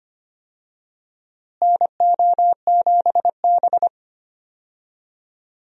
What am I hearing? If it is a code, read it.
Morse code, heard as a keyed tone, about 25 wpm, 700 Hz. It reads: NO7B
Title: USA Call Signs - 4 Characters - Repeated 2x - 25wpm